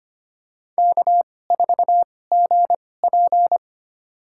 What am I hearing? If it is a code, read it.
Morse code, heard as a keyed tone, about 25 wpm, spaced as for 18 wpm, 700 Hz. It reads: K4GP
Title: USA Call Signs - 4 Characters - 2x Character Spacing - 25wpm